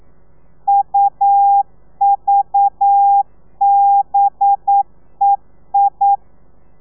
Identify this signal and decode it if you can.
Morse code, heard as a keyed tone, about 9 wpm, 800 Hz. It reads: UVBEI
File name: telegrafia / i2tfj / Lez4 40